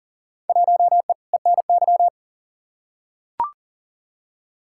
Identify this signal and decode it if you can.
Morse code, heard as a keyed tone, about 40 wpm, 700 Hz. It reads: 1E ENY
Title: Field Day Exchanges - 40wpm